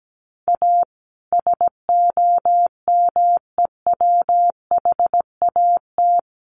Morse code, heard as a keyed tone, about 17 wpm, 700 Hz. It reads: A SOMEWHAT
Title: Sentences from Top 1300 Words 17wpm